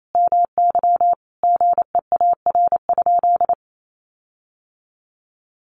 Morse code, heard as a keyed tone, about 28 wpm, 700 Hz. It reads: MY GEAR?
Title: Sentences from Top 1400 Words 28wpm